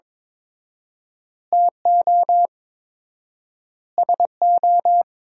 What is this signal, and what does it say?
Morse code, heard as a keyed tone, about 22 wpm, 700 Hz. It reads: TO SO
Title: 4x Word-Spacing - Sentences from Top 400 Words - 22wpm